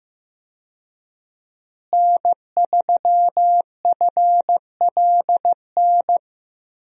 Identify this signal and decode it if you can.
Morse code, heard as a keyed tone, about 15 wpm, 700 Hz. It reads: N3FLN
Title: USA Call Signs - 5 Characters - Repeated 2x - 15wpm